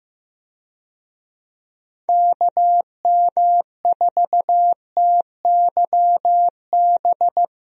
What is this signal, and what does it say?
Morse code, heard as a keyed tone, about 15 wpm, 700 Hz. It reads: KM4TYB
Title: USA Call Signs - 6 Characters - Repeated 2x - 15wpm